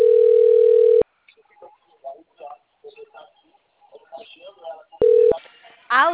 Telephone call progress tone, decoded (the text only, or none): ringback tone